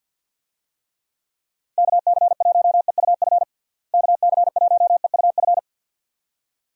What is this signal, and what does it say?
Morse code, heard as a keyed tone, about 52 wpm, 700 Hz. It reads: KC1EUF KC1EUF
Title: USA Call Signs - 6 Characters - Repeated 2x - 50wpm